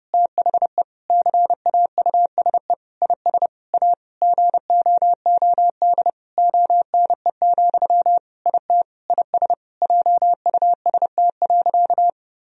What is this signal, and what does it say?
Morse code, heard as a keyed tone, about 31 wpm, 700 Hz. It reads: THE CAUSE IS A GOOD ONE, IT IS JUST.
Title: Sentences from Top 1000 Words - Review Part 1 - 30wpm